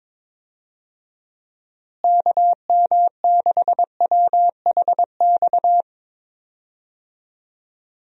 Morse code, heard as a keyed tone, about 22 wpm, 700 Hz. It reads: KM6WHX
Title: USA Call Signs - 6 Characters - Repeated 3x - 22wpm